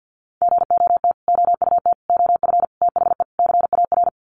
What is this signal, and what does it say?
Morse code, heard as a keyed tone, about 48 wpm, 700 Hz. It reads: GOT OUT OF THE CAR
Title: Sets of 5 Words Encore 50wpm